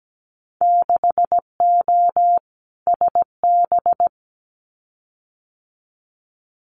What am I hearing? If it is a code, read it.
Morse code, heard as a keyed tone, about 17 wpm, 700 Hz. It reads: 6O SB
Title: Winter Field Day Exchanges Encore 17wpm